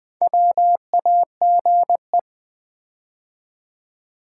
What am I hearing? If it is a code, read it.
Morse code, heard as a keyed tone, about 20 wpm, 700 Hz. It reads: WAGE
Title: Build A Word - 2 Letters At A Time - 4-Letter Words - 20wpm